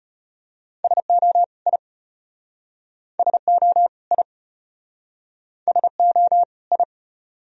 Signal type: Morse code

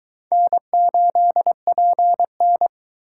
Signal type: Morse code

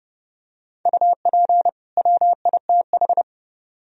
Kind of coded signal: Morse code

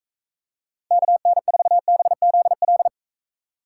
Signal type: Morse code

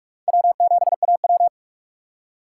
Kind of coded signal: Morse code

